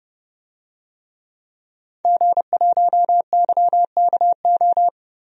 Morse code, heard as a keyed tone, about 32 wpm, 700 Hz. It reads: G1YKO
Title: DXCC-100 Call Signs - 5 Characters - Repeated 3x - 30wpm